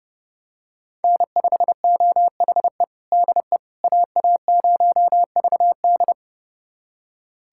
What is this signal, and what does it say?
Morse code, heard as a keyed tone, about 30 wpm, 700 Hz. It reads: N5OHE DE AA0VD
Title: USA Callsign DE Callsign - 5 Characters - 30wpm